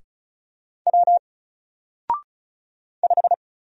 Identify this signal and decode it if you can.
Morse code, heard as a keyed tone, about 35 wpm, 700 Hz. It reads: W 5
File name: Single-Letter-Number-35wpm